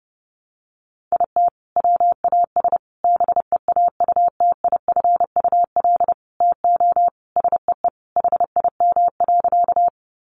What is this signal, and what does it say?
Morse code, heard as a keyed tone, about 30 wpm, 700 Hz. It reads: IT WAS BEAUTIFUL TO SEE HIM.
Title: Sentences from Top 900 Words 30wpm